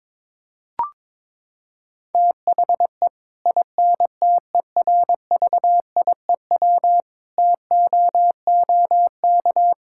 Morse code, heard as a keyed tone, about 22 wpm, 700 Hz. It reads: THE INTERVIEW TOOK
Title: Sentences from Top 1600 Words 22wpm